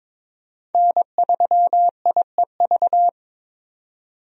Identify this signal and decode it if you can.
Morse code, heard as a keyed tone, about 22 wpm, 700 Hz. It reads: N3IEV
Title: USA Call Signs - 5 Characters - 22wpm